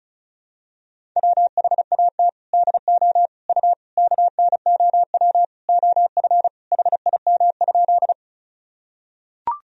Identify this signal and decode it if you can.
Morse code, heard as a keyed tone, about 36 wpm, 700 Hz. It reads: WHAT DO U KNOW OF HIM?
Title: Sentences from Top 100 Words and Abbreviations 35wpm